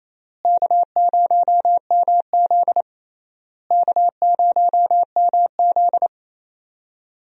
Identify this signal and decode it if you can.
Morse code, heard as a keyed tone, about 28 wpm, 700 Hz. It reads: K0MZ K0MZ
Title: USA Call Signs - 4 Characters - Repeated 3x - 28wpm